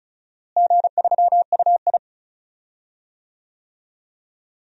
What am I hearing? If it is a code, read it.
Morse code, heard as a keyed tone, about 36 wpm, 700 Hz. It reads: G3UI
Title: DXCC-100 Call Signs - 4 Characters - Repeated 2x - 35wpm